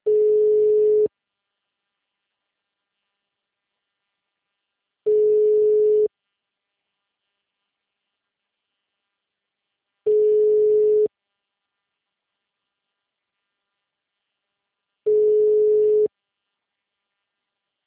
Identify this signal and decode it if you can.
Ringback tone (Europe, 425 Hz).
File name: out- -2003- - - .101